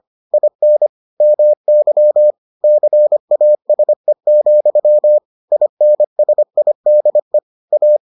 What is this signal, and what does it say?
Morse code, heard as a keyed tone, about 25 wpm, 600 Hz. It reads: IN MY CASE, INSIDE A